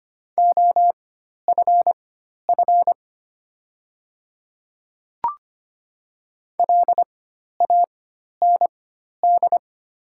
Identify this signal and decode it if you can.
Morse code, heard as a keyed tone, about 25 wpm, spaced as for 12 wpm, 700 Hz. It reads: OFF LAND
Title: 4x Character Spacing - Top 300 Words - 25wpm